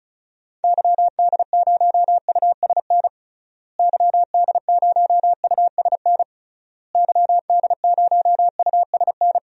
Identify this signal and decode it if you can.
Morse code, heard as a keyed tone, about 36 wpm, 700 Hz. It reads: YD0USN YD0USN YD0USN
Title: DXCC-100 Call Signs - 6 Characters - Repeated 3x - 35wpm